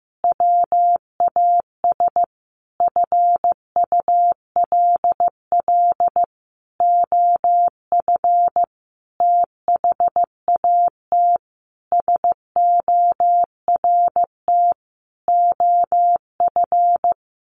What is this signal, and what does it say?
Morse code, heard as a keyed tone, about 15 wpm, 700 Hz. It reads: WAS FULL OF THAT SORT OF
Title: Sentences from Top 1300 Words 15wpm